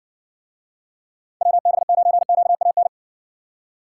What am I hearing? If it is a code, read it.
Morse code, heard as a keyed tone, about 59 wpm, 700 Hz. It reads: WB9QAN